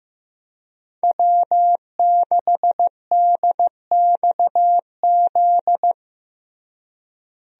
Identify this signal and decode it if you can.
Morse code, heard as a keyed tone, about 15 wpm, 700 Hz. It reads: W6DXZ
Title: USA Call Signs - 5 Characters - Repeated 2x - 15wpm